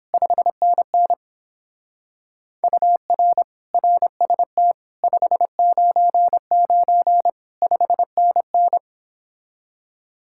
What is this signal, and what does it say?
Morse code, heard as a keyed tone, about 29 wpm, 700 Hz. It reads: 5NN UR RST 599 5NN